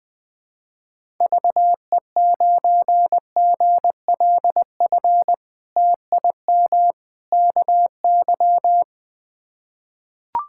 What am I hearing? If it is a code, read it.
Morse code, heard as a keyed tone, about 20 wpm, 700 Hz. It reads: VE9GLF TIM KY
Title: K1USN SST QSO Practice - 20wpm